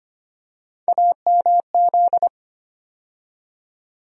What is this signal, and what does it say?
Morse code, heard as a keyed tone, about 25 wpm, 700 Hz. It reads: AMZ
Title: USA Call Sign Postfixes - 3 Letters 25wpm